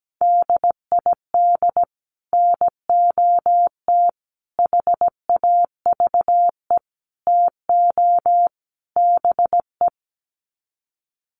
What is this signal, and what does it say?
Morse code, heard as a keyed tone, about 17 wpm, 700 Hz. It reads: DID NOT HAVE TO BE
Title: Sets of 5 Words Encore2 17wpm